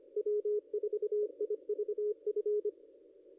Morse code, heard as a keyed tone, about 23 wpm, 410 Hz. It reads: W4IVF
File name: contests / ss-cw-k5tr-op-n5rz -0200